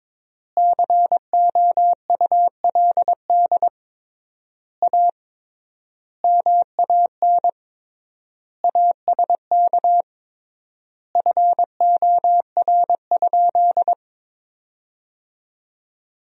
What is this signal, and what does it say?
Morse code, heard as a keyed tone, about 22 wpm, 700 Hz. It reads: COULD A MAN ASK FOR?
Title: 3x Word-Spacing - Sentences from Top 200 Words - 22wpm